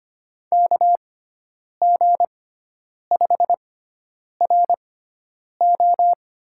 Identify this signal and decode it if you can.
Morse code, heard as a keyed tone, about 25 wpm, spaced as for 9 wpm, 700 Hz. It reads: KG5RO
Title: USA Call Signs - 6 Characters - 6x Character Spacing - 25wpm